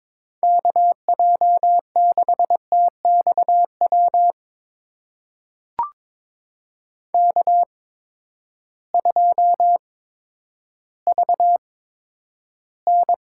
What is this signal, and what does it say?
Morse code, heard as a keyed tone, about 21 wpm, spaced as for 8 wpm, 700 Hz. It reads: KJ6TX WK2VN
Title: USA Call Signs - 8x Character Spacing - 22wpm